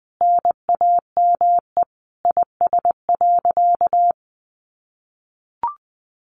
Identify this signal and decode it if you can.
Morse code, heard as a keyed tone, about 20 wpm, 700 Hz. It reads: NAME IS.
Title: Sentences from Top 1100 Words 20wpm